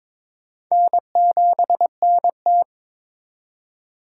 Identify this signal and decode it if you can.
Morse code, heard as a keyed tone, about 22 wpm, 700 Hz. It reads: N7NT
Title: K1USN SST QSO Practice - 22wpm